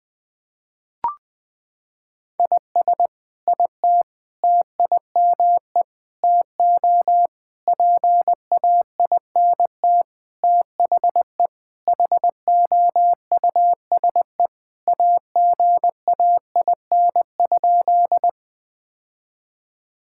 Morse code, heard as a keyed tone, about 20 wpm, 700 Hz. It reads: IS IT TIME TO PAINT THE HOUSE AGAIN?
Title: Sentences from Top 1000 Words - Review Part 6 - 20wpm